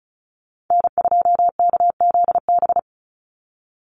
Morse code, heard as a keyed tone, about 34 wpm, 700 Hz. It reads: N2KZB
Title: USA Call Signs 35wpm